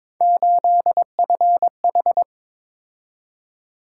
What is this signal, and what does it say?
Morse code, heard as a keyed tone, about 21 wpm, 700 Hz. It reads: 8FH
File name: practice / 22wpm / Three Letter-Number Combinations - 22wpm